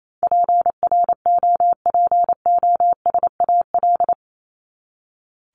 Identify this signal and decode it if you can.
Morse code, heard as a keyed tone, about 28 wpm, 700 Hz. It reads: PROPOSAL
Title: Top 1751-2000 Words 28wpm